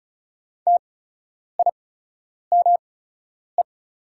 Morse code, heard as a keyed tone, about 36 wpm, spaced as for 10 wpm, 690 Hz. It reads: TIME
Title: 8x Character Spacing - Top 200 Words - 35wpm